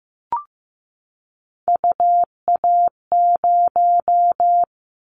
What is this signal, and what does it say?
Morse code, heard as a keyed tone, about 15 wpm, 700 Hz. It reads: UA0
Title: Callsign Prefixes for Top 50 DXCC - 15wpm